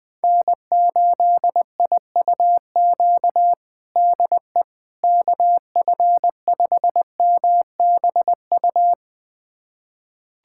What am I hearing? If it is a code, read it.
Morse code, heard as a keyed tone, about 21 wpm, 700 Hz. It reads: N8IUQ DE KF5MBU
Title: USA Callsign DE Callsign 20wpm